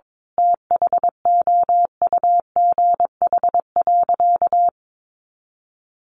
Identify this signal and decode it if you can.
Morse code, heard as a keyed tone, about 22 wpm, 700 Hz. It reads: THOUGH.
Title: Sentences from Top 1500 Words 22wpm